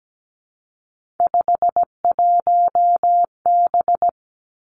Morse code, heard as a keyed tone, about 17 wpm, 700 Hz. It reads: 51B
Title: Three Challenging Letter-Number Combinations 17wpm